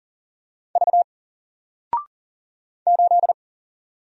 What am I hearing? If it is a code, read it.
Morse code, heard as a keyed tone, about 40 wpm, 700 Hz. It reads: V 8
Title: Single Letter-Number 40wpm